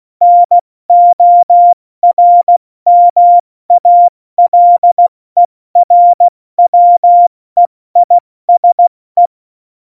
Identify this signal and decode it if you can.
Morse code, heard as a keyed tone, about 16 wpm, 700 Hz. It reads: NORMALERWEISE
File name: Morsekurs / Witze / Witz 30. Oktober 2018, 70 F80